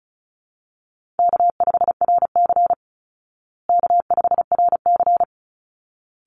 Morse code, heard as a keyed tone, about 34 wpm, 700 Hz. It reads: K5RC K5RC
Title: Easy Call Signs Repeated 2x - 35wpm